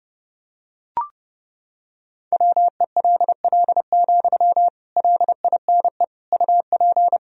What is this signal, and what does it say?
Morse code, heard as a keyed tone, about 32 wpm, 700 Hz. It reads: WELL, LINE UP
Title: Sentences from Top 1000 Words - Review Part 1 - 30wpm